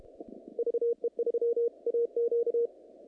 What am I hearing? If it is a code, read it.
Morse code, heard as a keyed tone, about 31 wpm, 470 Hz. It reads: VE3AQ
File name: contests / naqp-cw-k5tr-op-k5pi- -2100